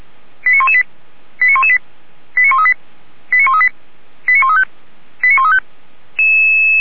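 Five-tone selective call (ZVEI), heard as quick runs of five tones, tones of about 70 ms. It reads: 89118 89118 89138 89138 89135 89135
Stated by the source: ZVEI five-tone selective call received with an SDR, from the Feuerwehr Orlamünde audio archive.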